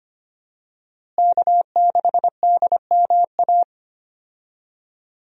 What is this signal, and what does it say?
Morse code, heard as a keyed tone, about 25 wpm, 700 Hz. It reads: K6DMA
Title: USA Call Signs - 5 Characters - Repeated 2x - 25wpm